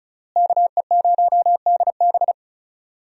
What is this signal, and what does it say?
Morse code, heard as a keyed tone, about 36 wpm, 700 Hz. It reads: KE0DB
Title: USA Call Signs - 5 Characters - Repeated 3x - 35wpm